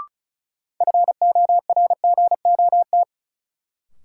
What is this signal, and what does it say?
Morse code, heard as a keyed tone, about 36 wpm, 700 Hz. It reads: FORGOT
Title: Sentences from Top 1400 Words - New Word Drill - 35wpm